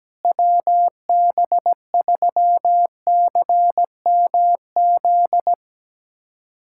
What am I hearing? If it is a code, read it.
Morse code, heard as a keyed tone, about 17 wpm, 700 Hz. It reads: WB3CMZ
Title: USA Callsign DE Callsign 17wpm